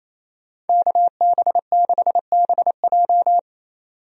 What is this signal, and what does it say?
Morse code, heard as a keyed tone, about 28 wpm, 700 Hz. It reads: KB6BJ
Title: USA Call Signs - 5 Characters - 28wpm